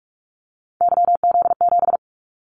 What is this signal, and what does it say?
Morse code, heard as a keyed tone, about 44 wpm, 700 Hz. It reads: YZ7
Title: Callsign Prefixes for Top 25 DXCC - Repeated 3x - 45wpm